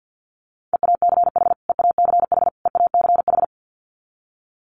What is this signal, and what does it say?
Morse code, heard as a keyed tone, about 50 wpm, 700 Hz. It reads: EACH EACH EACH
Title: Top 300 Words Repeated 50wpm